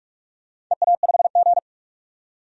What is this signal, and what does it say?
Morse code, heard as a keyed tone, about 47 wpm, 700 Hz. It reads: EA5G
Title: DXCC-100 Call Signs - 4 Characters - 45wpm